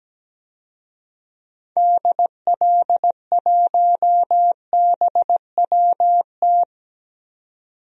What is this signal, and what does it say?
Morse code, heard as a keyed tone, about 17 wpm, 700 Hz. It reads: DL1BWT
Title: DXCC-100 Call Signs - 6 Characters - 17wpm